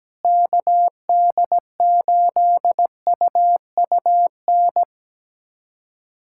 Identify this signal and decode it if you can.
Morse code, heard as a keyed tone, about 17 wpm, 700 Hz. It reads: KD8UUN
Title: USA Callsign DE Callsign - 6 Characters - 17wpm